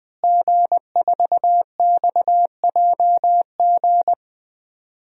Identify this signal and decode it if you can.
Morse code, heard as a keyed tone, about 20 wpm, 700 Hz. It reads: G4XJG
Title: DXCC-100 Call Signs - 5 Characters - Repeated 3x - 20wpm